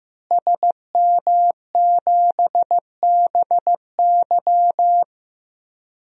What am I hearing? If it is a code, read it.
Morse code, heard as a keyed tone, about 16 wpm, 700 Hz. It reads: SM7BY
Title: DXCC-100 Call Signs - 5 Characters - Repeated 2x - 15wpm